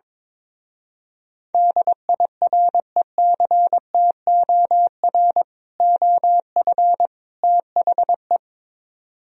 Morse code, heard as a keyed tone, about 22 wpm, 700 Hz. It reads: DIRECTOR OF THE
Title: Speed-Racing - Sets of 3 Words 22wpm